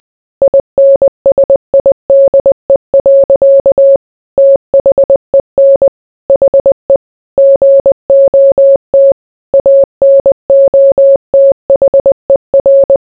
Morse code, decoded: INSIDE. THEN HE GOT ANOTHER